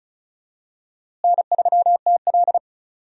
Morse code, heard as a keyed tone, about 36 wpm, 700 Hz. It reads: N3TL
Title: USA Call Signs - 4 Characters - Repeated 2x - 35wpm